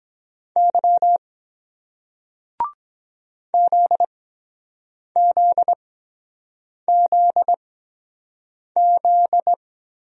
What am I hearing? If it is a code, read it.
Morse code, heard as a keyed tone, about 23 wpm, spaced as for 7 wpm, 700 Hz. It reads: Y ZZZZ